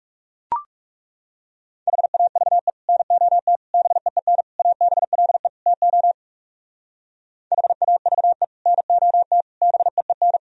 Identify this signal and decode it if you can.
Morse code, heard as a keyed tone, about 38 wpm, 700 Hz. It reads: HAVE NOT BEEN ABLE TO HAVE NOT BEEN